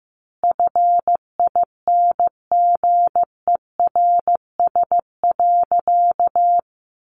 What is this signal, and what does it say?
Morse code, heard as a keyed tone, about 15 wpm, 700 Hz. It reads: FINGERS.
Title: A Sound Of Thunder 15wpm